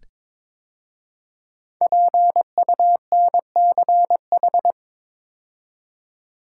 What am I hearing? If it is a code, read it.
Morse code, heard as a keyed tone, about 22 wpm, 690 Hz. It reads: PUNCH
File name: 5 Letter Words - First 3 Letters Given - 22wpm